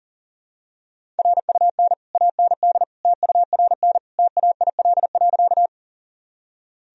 Morse code, heard as a keyed tone, about 40 wpm, 700 Hz. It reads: RUN AND TURN TAIL.
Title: Sentences from Top 1000 Words - Review Part 3 - 40wpm